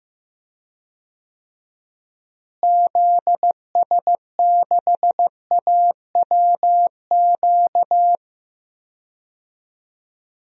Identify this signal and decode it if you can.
Morse code, heard as a keyed tone, about 15 wpm, 700 Hz. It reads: ZS6AWQ
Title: DXCC-100 Call Signs - 6 Characters - Repeated 3x - 15wpm